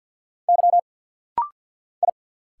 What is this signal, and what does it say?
Morse code, heard as a keyed tone, about 50 wpm, 690 Hz. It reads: Y I
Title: Single Letters - Rapid-Fire 50wpm